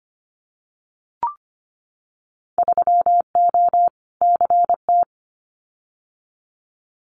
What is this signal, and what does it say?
Morse code, heard as a keyed tone, about 25 wpm, 700 Hz. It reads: 3O CT
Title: Winter Field Day Exchanges Encore 25wpm